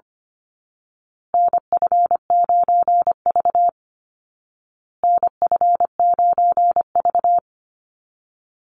Morse code, heard as a keyed tone, about 25 wpm, 700 Hz. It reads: NF9V NF9V
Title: Call Signs Repeated 2x - 25wpm